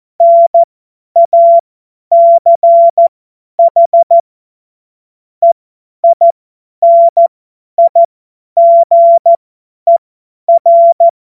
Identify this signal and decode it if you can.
Morse code, heard as a keyed tone, about 14 wpm, spaced as for 10 wpm, 680 Hz. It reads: NACH EINIGER
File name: Morsekurs / Witze / Witz 20. Dezember 2022 50 F70